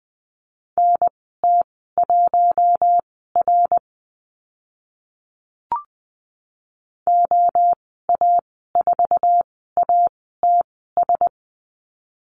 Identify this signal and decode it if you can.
Morse code, heard as a keyed tone, about 20 wpm, spaced as for 15 wpm, 700 Hz. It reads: NT1R OA4ATS
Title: 082 Lesson 14g - Number 2 - Call Signs 20wpm